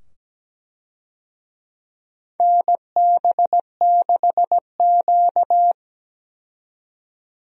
Morse code, heard as a keyed tone, about 17 wpm, 700 Hz. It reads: NB6Q
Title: USA Call Signs - 4 Characters - 17wpm